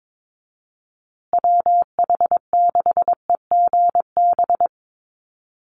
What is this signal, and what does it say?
Morse code, heard as a keyed tone, about 22 wpm, 700 Hz. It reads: WH6EGB
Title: USA Call Signs Repeated 2x 22wpm